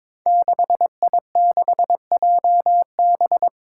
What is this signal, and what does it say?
Morse code, heard as a keyed tone, about 22 wpm, 700 Hz. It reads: 6I6JB
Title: Five Challenging Letter-Number Combinations 22wpm